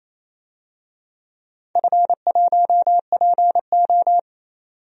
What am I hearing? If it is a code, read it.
Morse code, heard as a keyed tone, about 28 wpm, 700 Hz. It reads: F1PO
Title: DXCC-100 Call Signs - 4 Characters - 28wpm